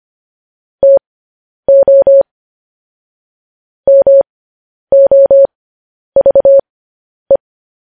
Morse code, heard as a keyed tone, about 25 wpm, spaced as for 10 wpm, 550 Hz. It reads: TO MOVE
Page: news headlines in morse code at 10 wpm